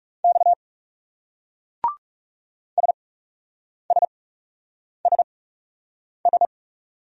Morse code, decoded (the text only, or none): X SSSS